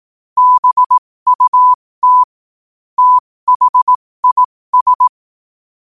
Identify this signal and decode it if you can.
Morse code, heard as a keyed tone, about 18 wpm, spaced as for 15 wpm, 1000 Hz. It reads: BUT THIS
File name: qotdfiles / 15wpm